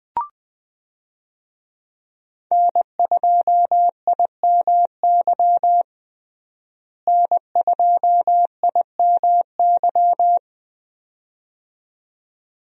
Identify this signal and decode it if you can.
Morse code, heard as a keyed tone, about 21 wpm, 700 Hz. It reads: N2IMY N2IMY
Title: USA Call Signs - 5 Characters - Repeated 2x - 20wpm